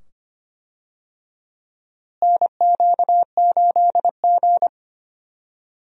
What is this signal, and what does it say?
Morse code, heard as a keyed tone, about 25 wpm, 700 Hz. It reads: NQ8G
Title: USA Call Signs - 4 Characters - Repeated 2x - 25wpm